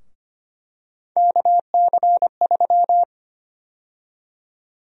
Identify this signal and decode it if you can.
Morse code, heard as a keyed tone, about 25 wpm, 700 Hz. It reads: KC3
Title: USA Call Sign Prefixes - Repeated 2x - 25wpm